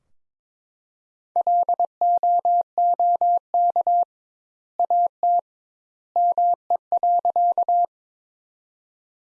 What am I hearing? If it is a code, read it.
Morse code, heard as a keyed tone, about 22 wpm, 700 Hz. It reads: LOOK AT ME.
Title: Sentences Top 200 Words - 22wpm